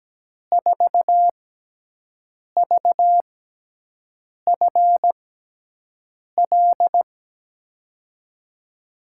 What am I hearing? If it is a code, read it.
Morse code, heard as a keyed tone, about 17 wpm, spaced as for 6 wpm, 700 Hz. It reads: 4VFL
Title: USA Call Signs - 5 Characters - 6x Character Spacing - 17wpm